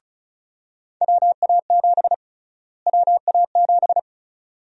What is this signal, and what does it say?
Morse code, heard as a keyed tone, about 36 wpm, 700 Hz. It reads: WA7 WA7
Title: USA Call Sign Prefixes - Repeated 2x - 35wpm